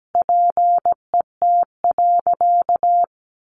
Morse code, heard as a keyed tone, about 17 wpm, 700 Hz. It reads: PET.
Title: Additional Easy Phrases 17wpm